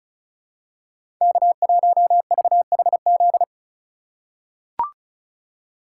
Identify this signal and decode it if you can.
Morse code, heard as a keyed tone, about 36 wpm, 700 Hz. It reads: K1VHZ
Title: USA Call Signs - 3x Character Spacing - 35wpm